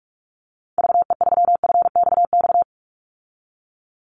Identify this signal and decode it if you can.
Morse code, heard as a keyed tone, about 44 wpm, 700 Hz. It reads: VE3FXX
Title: Call Signs Repeated 3x - 45wpm